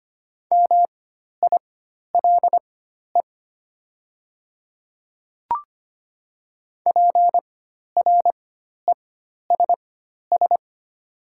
Morse code, heard as a keyed tone, about 25 wpm, spaced as for 12 wpm, 690 Hz. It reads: MILE PRESS
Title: 4x Character Spacing - Top 300 Words - 25wpm